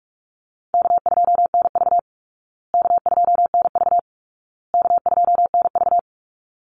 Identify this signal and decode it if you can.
Morse code, heard as a keyed tone, about 44 wpm, 700 Hz. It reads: K2NV K2NV K2NV
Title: Easy Call Signs Repeated 3x - 45wpm